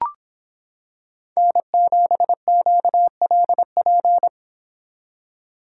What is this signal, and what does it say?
Morse code, heard as a keyed tone, about 25 wpm, 700 Hz. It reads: N7QLP